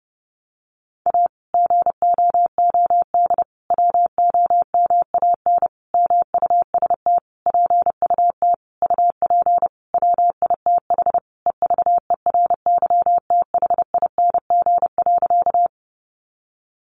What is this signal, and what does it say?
Morse code, heard as a keyed tone, about 30 wpm, 700 Hz. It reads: A GOOD WOMAN MUST PUT UP WITH EVERYTHING.
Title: Sentences from Top 1100 Words 30wpm